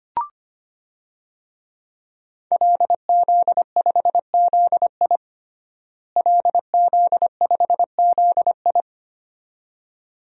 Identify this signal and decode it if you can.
Morse code, heard as a keyed tone, about 25 wpm, 700 Hz. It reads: LZ5ZI LZ5ZI
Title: DXCC-100 Call Signs - 5 Characters - Repeated 2x - 25wpm